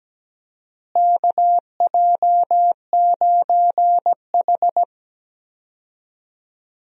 Morse code, heard as a keyed tone, about 17 wpm, 700 Hz. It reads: KJ9H
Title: USA Call Signs - 4 Characters - Repeated 2x - 17wpm